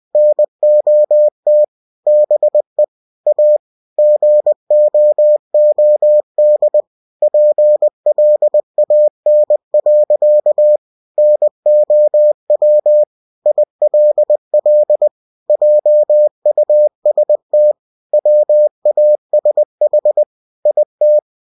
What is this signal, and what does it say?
Morse code, heard as a keyed tone, about 20 wpm, 600 Hz. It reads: NOT BE A GOOD PLAN. NOW ILL JUST WASH IT
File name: cw-books / winnie-the-pooh / winnie-20wpm-page-0051